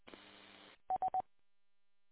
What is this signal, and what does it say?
There is no code.